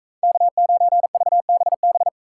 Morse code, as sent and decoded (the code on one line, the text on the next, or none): -.- ----. ...- -... -...
K9VBB